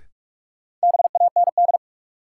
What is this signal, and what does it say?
Morse code, heard as a keyed tone, about 46 wpm, 700 Hz. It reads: BAND
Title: Sentences from Top 1600 Words - New Word Drill - 45wpm